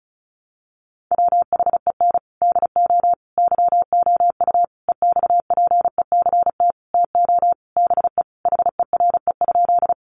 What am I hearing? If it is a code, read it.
Morse code, heard as a keyed tone, about 34 wpm, 700 Hz. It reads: WHEN DO YOU EXPECT TO BE HERE?
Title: Sentences from Top 1100 Words 35wpm